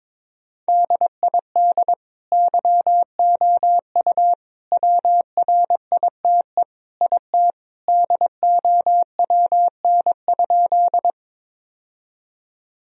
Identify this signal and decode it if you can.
Morse code, heard as a keyed tone, about 21 wpm, 700 Hz. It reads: DID YOU WRITE IT DOWN?
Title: Sentences from Top 1000 Words - Review Part 1 - 22wpm